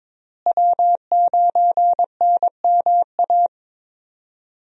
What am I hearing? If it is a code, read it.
Morse code, heard as a keyed tone, about 22 wpm, 700 Hz. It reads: W9NMA
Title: USA Call Signs - 5 Characters - Repeated 2x - 22wpm